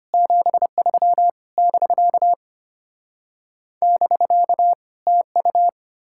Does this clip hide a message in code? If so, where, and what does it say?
Yes: Morse code from 0.1 to 5.7 s, heard as a keyed tone, about 33 wpm, 700 Hz. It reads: 73 <BK> <BK> TU